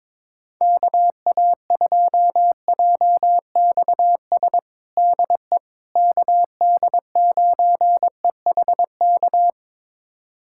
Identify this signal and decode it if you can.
Morse code, heard as a keyed tone, about 22 wpm, 700 Hz. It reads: KA2JXS DE KD9EHK
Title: USA Callsign DE Callsign - 6 Characters - 22wpm